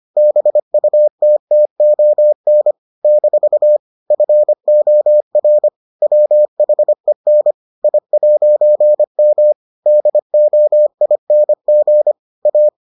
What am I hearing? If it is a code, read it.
Morse code, heard as a keyed tone, about 25 wpm, 600 Hz. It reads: BUTTON - FOR WHEN I'M DOING A